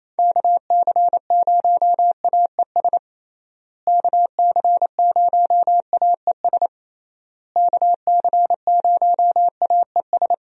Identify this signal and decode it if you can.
Morse code, heard as a keyed tone, about 28 wpm, 700 Hz. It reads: KC0AES KC0AES KC0AES
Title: USA Call Signs - 6 Characters - Repeated 3x - 28wpm